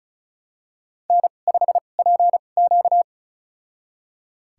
Morse code, heard as a keyed tone, about 36 wpm, spaced as for 25 wpm, 700 Hz. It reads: N5PQ
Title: USA Call Signs - 4 Characters - 2x Character Spacing - 35wpm